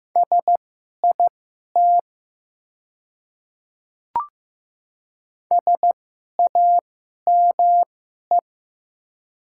Morse code, heard as a keyed tone, about 15 wpm, spaced as for 11 wpm, 700 Hz. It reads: SIT SAME
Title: 2x Character Spacing - Top 500 Words - 15wpm